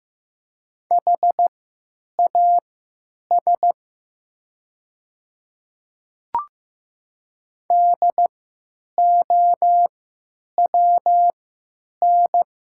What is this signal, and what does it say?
Morse code, heard as a keyed tone, about 15 wpm, spaced as for 9 wpm, 700 Hz. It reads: HAS DOWN
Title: 3x Character Spacing - Top 100 Words - 15wpm